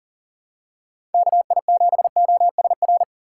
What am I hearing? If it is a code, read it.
Morse code, heard as a keyed tone, about 40 wpm, 700 Hz. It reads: KI7OSR
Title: USA Call Signs - 6 Characters - 3x Character Spacing - 40wpm